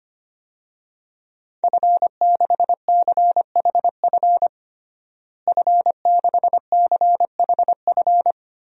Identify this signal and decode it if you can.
Morse code, heard as a keyed tone, about 25 wpm, 700 Hz. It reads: F6CHF F6CHF
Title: DXCC-100 Call Signs - 5 Characters - Repeated 3x - 25wpm